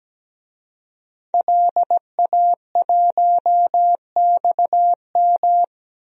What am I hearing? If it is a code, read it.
Morse code, heard as a keyed tone, about 17 wpm, 700 Hz. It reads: LA1XM
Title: DXCC-100 Call Signs - 5 Characters - 17wpm